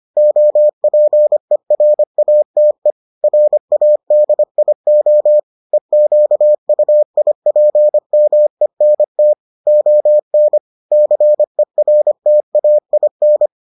Morse code, heard as a keyed tone, about 25 wpm, 600 Hz. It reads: OPERATE RADIO EQUIPMENT ON CERTAIN